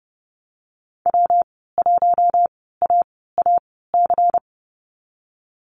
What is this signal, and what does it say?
Morse code, heard as a keyed tone, about 32 wpm, spaced as for 17 wpm, 700 Hz. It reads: W1AAC
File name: Lesson 19g - Letter w - Call Signs 30wpm